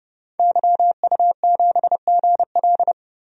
Morse code, heard as a keyed tone, about 30 wpm, 700 Hz. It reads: YU7GL
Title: DXCC-100 Call Signs - 5 Characters - 30wpm